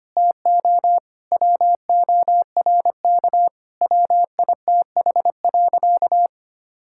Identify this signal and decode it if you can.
Morse code, heard as a keyed tone, about 25 wpm, 700 Hz. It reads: TO WORK WITH.
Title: Sentences from Top 1000 Words - Review Part 5 - 25wpm